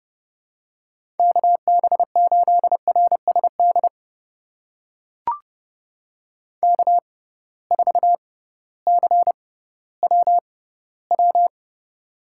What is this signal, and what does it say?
Morse code, heard as a keyed tone, about 30 wpm, spaced as for 16 wpm, 700 Hz. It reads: KB8RSD K4CWW
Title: USA Call Signs - 6x Character Spacing - 30wpm